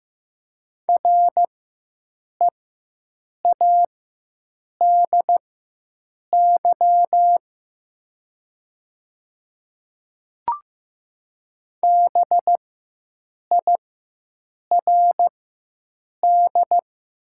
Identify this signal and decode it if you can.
Morse code, heard as a keyed tone, about 15 wpm, spaced as for 7 wpm, 700 Hz. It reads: READY BIRD
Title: 4x Character Spacing - Top 500 Words - 15wpm